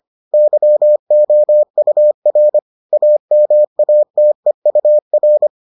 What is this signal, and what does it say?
Morse code, heard as a keyed tone, about 25 wpm, 600 Hz. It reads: YOUR AMATEUR